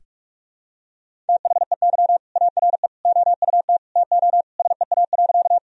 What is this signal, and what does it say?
Morse code, heard as a keyed tone, about 46 wpm, 700 Hz. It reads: THEY ARE OUT TO SEA.
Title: Sentences from Top 1000 Words - Review Part 1 - 45wpm